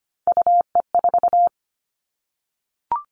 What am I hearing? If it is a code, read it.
Morse code, heard as a keyed tone, about 25 wpm, 700 Hz. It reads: UE4
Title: Three Challenging Letter-Number Combinations 25wpm